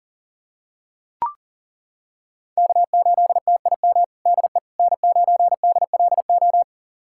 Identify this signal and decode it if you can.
Morse code, heard as a keyed tone, about 40 wpm, 700 Hz. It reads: K8TIM DE N9DLO
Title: USA Callsign DE Callsign - 5 Characters - 40wpm